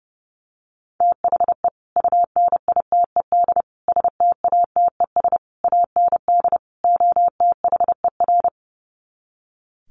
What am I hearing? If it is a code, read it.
Morse code, heard as a keyed tone, about 30 wpm, 700 Hz. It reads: THE UNITED STATES AND OTHER
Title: Sets of 5 Words Encore 30wpm